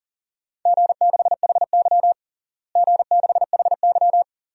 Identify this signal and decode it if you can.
Morse code, heard as a keyed tone, about 40 wpm, 700 Hz. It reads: G6HY G6HY
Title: DXCC-100 Call Signs - 4 Characters - Repeated 2x - 40wpm